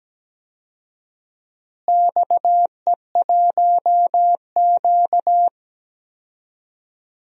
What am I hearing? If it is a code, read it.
Morse code, heard as a keyed tone, about 17 wpm, 700 Hz. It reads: XE1Q
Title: DXCC-100 Call Signs - 4 Characters - Repeated 2x - 17wpm